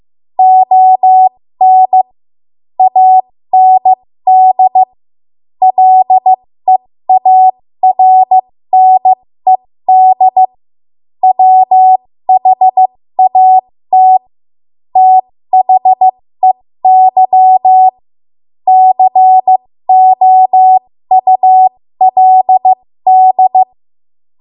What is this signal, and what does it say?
Morse code, heard as a keyed tone, about 15 wpm, spaced as for 13 wpm, 750 Hz. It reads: ON AND LEARNED WHAT THEY COULD
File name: Morse / 13WPM